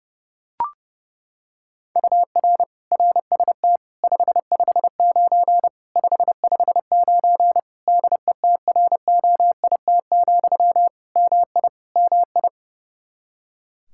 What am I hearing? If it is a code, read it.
Morse code, heard as a keyed tone, about 32 wpm, 700 Hz. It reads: UR RST 559 559 DETROIT, MI MI
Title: Practice QSO - Exchange 1 - Beginning - 30wpm